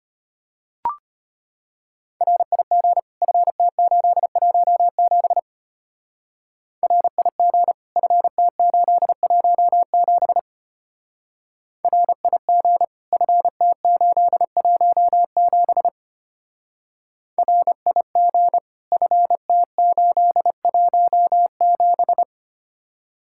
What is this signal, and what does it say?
Morse code, heard as a keyed tone, about 31 wpm, 700 Hz. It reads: RIG FT817 RIG FT817 RIG FT817 RIG FT817